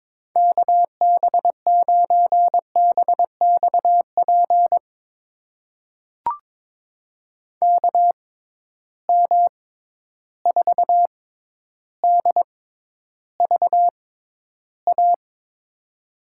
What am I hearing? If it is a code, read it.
Morse code, heard as a keyed tone, about 21 wpm, spaced as for 11 wpm, 700 Hz. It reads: KB9BXP KM4DVA
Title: USA Call Signs - 6 Characters - 6x Character Spacing - 22wpm